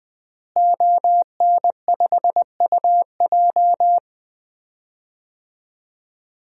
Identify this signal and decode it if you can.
Morse code, heard as a keyed tone, about 21 wpm, 700 Hz. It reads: ON5UJ
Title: DXCC-100 Call Signs - 5 Characters - Repeated 2x - 20wpm